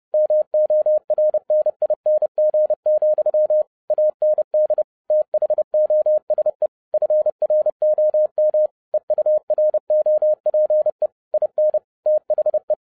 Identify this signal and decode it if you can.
Morse code, heard as a keyed tone, about 30 wpm, 610 Hz. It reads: MORNING, AND THOSE FROM EUROPE IN THE
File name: 2012-01-mp3 / n12-I DIDNT MISS THE SOLAR CYCLE 24-30wpm